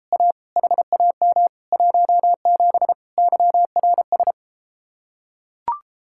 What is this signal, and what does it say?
Morse code, heard as a keyed tone, about 34 wpm, 700 Hz. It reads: A HAM 17 YRS